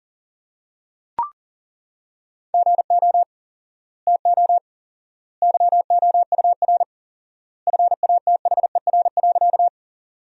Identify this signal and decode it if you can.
Morse code, heard as a keyed tone, about 40 wpm, 700 Hz. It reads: GO TO YOUR FATHER.
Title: 4x Word-Spacing - Sentences from Top 300 Words - 40wpm